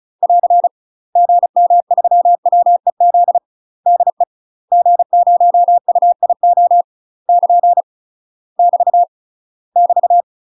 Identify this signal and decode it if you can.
Morse code, heard as a keyed tone, about 35 wpm, 700 Hz. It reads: <AR> GM3WEZ DE G0UIO <KN> <BT> <BT>
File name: UK MASTER QSOs TRAINING / 35WPM / UK MASTER QSOs 0026